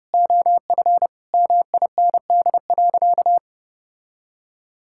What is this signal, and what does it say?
Morse code, heard as a keyed tone, about 32 wpm, 700 Hz. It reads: OF MIND.
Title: Sentences from Top 1000 Words - Review Part 6 - 30wpm